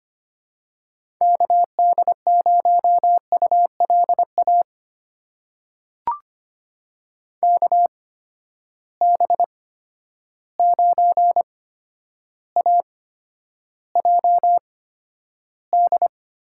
Morse code, heard as a keyed tone, about 25 wpm, spaced as for 11 wpm, 700 Hz. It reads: KD0ULA KB9AJD
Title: USA Call Signs - 6 Characters - 8x Character Spacing - 25wpm